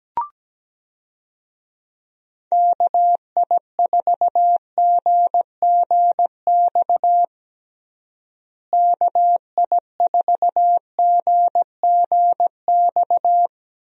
Morse code, heard as a keyed tone, about 17 wpm, 700 Hz. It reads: KI4GGX KI4GGX
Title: USA Call Signs - 6 Characters - Repeated 3x - 17wpm